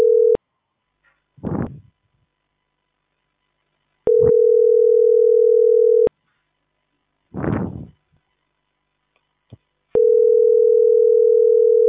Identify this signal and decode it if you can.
Ringback tone (North America), 440 + 480 Hz.